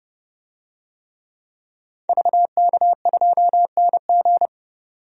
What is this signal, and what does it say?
Morse code, heard as a keyed tone, about 30 wpm, 700 Hz. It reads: VK2NG
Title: DXCC-100 Call Signs - 5 Characters - 30wpm